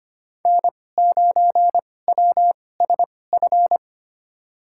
Morse code, heard as a keyed tone, about 25 wpm, spaced as for 18 wpm, 700 Hz. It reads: N9WSF
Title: USA Call Signs - 5 Characters - 2x Character Spacing - 25wpm